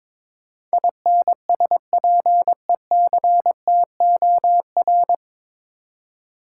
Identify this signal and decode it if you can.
Morse code, heard as a keyed tone, about 22 wpm, 700 Hz. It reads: INSPECTOR
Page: Morse Code Ninja practice 22wpm